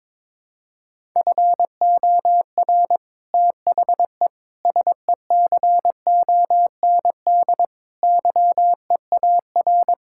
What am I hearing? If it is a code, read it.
Morse code, heard as a keyed tone, about 22 wpm, 700 Hz. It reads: FOR THE SECOND YEAR
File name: practice / 22wpm / Sets of 8 Words - 22wpm